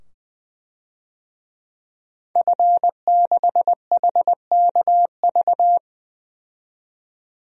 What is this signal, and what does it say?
Morse code, heard as a keyed tone, about 21 wpm, 700 Hz. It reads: F6HKV
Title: DXCC-100 Call Signs - 5 Characters - 20wpm